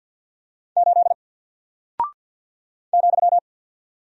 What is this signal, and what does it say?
Morse code, heard as a keyed tone, about 49 wpm, 700 Hz. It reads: 8 ,
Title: Single Character (Letter:Number:Punctuation:Prosign) Alt 50wpm